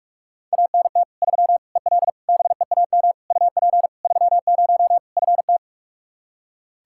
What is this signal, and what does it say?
Morse code, heard as a keyed tone, about 46 wpm, 700 Hz. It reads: ANT 3 EL BEAM UP 30 FT
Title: Practice QSO - Exchange 2 - Info - 45wpm